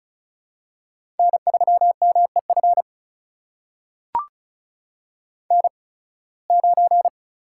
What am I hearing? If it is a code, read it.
Morse code, heard as a keyed tone, about 36 wpm, spaced as for 23 wpm, 690 Hz. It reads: N3MEF N9
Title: USA Call Signs - 5 Characters - 8x Character Spacing - 35wpm